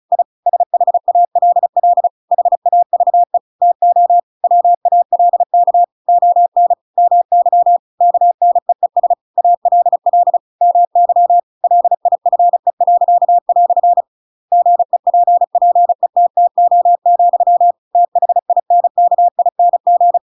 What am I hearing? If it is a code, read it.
Morse code, heard as a keyed tone, about 34 wpm, 700 Hz. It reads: I SHALL HAVE TO WALK ON MY KNEES ALL MY LIFE." GEPPETTO, THINKING
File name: Pinocchio / Pinocchio ch7 35 35 ik1hge